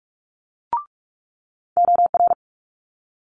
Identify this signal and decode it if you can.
Morse code, heard as a keyed tone, about 46 wpm, 700 Hz. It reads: OR
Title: Multi-letter Phonemes Quick Fire 45wpm